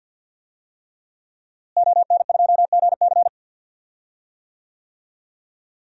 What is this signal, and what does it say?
Morse code, heard as a keyed tone, about 46 wpm, 700 Hz. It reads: ON2GC
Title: DXCC-100 Call Signs - 5 Characters - Repeated 3x - 50wpm